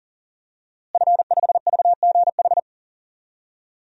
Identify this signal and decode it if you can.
Morse code, heard as a keyed tone, about 40 wpm, 700 Hz. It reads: F5VGH
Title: DXCC-100 Call Signs - 5 Characters - Repeated 3x - 40wpm